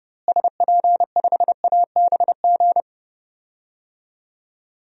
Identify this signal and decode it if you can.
Morse code, heard as a keyed tone, about 30 wpm, 700 Hz. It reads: SP5ABG
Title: DXCC-100 Call Signs - 6 Characters - Repeated 2x - 30wpm